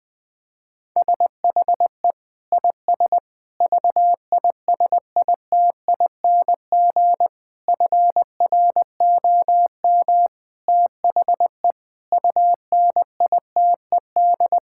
Morse code, heard as a keyed tone, about 20 wpm, 700 Hz. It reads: SHE IS VISITING FROM THE UNITED
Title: Sentences with 4-Word Phrases - Spoken Beginning - 20wpm